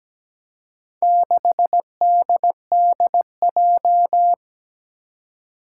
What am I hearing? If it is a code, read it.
Morse code, heard as a keyed tone, about 17 wpm, 700 Hz. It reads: 6DDJ
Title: Four Challenging Letter-Number Combinations 17wpm